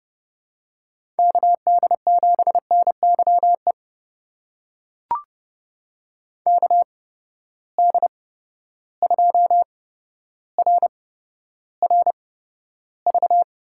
Morse code, heard as a keyed tone, about 30 wpm, spaced as for 13 wpm, 700 Hz. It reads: KD7NYE KD2RRV
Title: USA Call Signs - 6 Characters - 8x Character Spacing - 30wpm